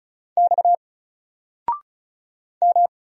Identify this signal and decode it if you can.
Morse code, heard as a keyed tone, about 36 wpm, 700 Hz. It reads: X M
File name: Single-Letter-Number-35wpm